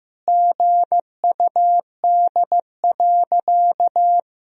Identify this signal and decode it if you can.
Morse code, heard as a keyed tone, about 15 wpm, 700 Hz. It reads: GUD.
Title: Sentences from Top 200 Words and Abbreviations 15wpm